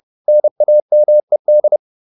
Morse code, heard as a keyed tone, about 30 wpm, 600 Hz. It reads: NAMED